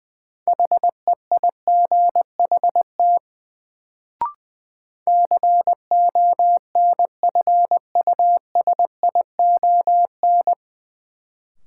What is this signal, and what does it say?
Morse code, heard as a keyed tone, about 20 wpm, 700 Hz. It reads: HEIGHT CONFUSION
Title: Sentences from Top 1600 Words - New Word Drill - 20wpm